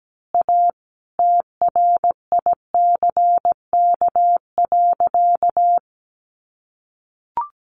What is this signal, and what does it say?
Morse code, heard as a keyed tone, about 17 wpm, 700 Hz. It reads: A TRICK.
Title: Sentences from Top 1500 Words 17wpm